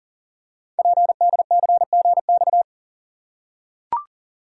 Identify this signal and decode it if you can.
Morse code, heard as a keyed tone, about 39 wpm, 690 Hz. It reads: PDCGX
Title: Code Groups - 5 Letters - 40wpm